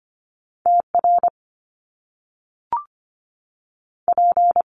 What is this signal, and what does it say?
Morse code, heard as a keyed tone, about 25 wpm, 700 Hz. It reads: TR P
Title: Top Two Letter Combinations in Words 25wpm